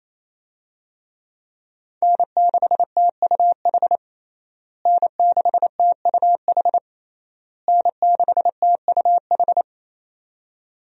Morse code, heard as a keyed tone, about 28 wpm, 700 Hz. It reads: N6TUH N6TUH N6TUH
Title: USA Call Signs - 5 Characters - Repeated 3x - 28wpm